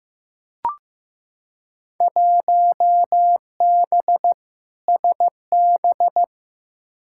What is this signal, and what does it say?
Morse code, heard as a keyed tone, about 16 wpm, 700 Hz. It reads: 1B SB
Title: Field Day Exchanges - 15wpm